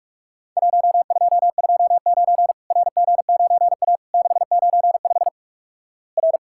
Morse code, heard as a keyed tone, about 46 wpm, 700 Hz. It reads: 1229 RG9A 695 R
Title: Worked All Europe DX Contest - QTC Exchange - 45wpm